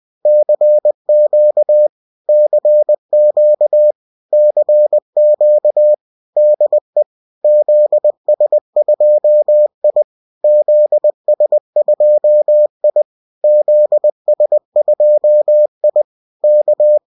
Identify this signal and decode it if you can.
Morse code, heard as a keyed tone, about 20 wpm, 600 Hz. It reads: CQ CQ CQ DE ZS2I ZS2I ZS2I K